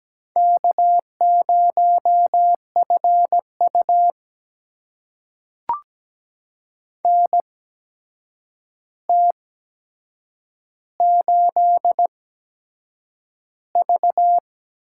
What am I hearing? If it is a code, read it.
Morse code, heard as a keyed tone, about 17 wpm, spaced as for 7 wpm, 700 Hz. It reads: K0FU NT8V
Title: USA Call Signs - 4 Characters - 8x Character Spacing - 17wpm